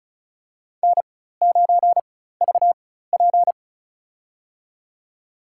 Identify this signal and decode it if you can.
Morse code, heard as a keyed tone, about 36 wpm, spaced as for 16 wpm, 700 Hz. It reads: N9VP
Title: USA Call Signs - 4 Characters - 4x Character Spacing - 35wpm